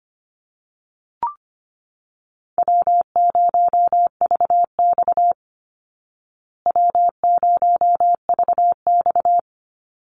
Morse code, heard as a keyed tone, about 25 wpm, 700 Hz. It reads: W0VX W0VX
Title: Easy Call Signs Repeated 3x - 25wpm